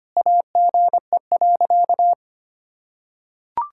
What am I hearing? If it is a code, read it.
Morse code, heard as a keyed tone, about 25 wpm, 700 Hz. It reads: AGE.
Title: Sentences from Top 1600 Words 25wpm